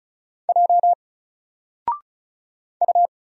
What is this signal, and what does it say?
Morse code, heard as a keyed tone, about 35 wpm, 700 Hz. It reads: J U
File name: Single-Letter-Number-35wpm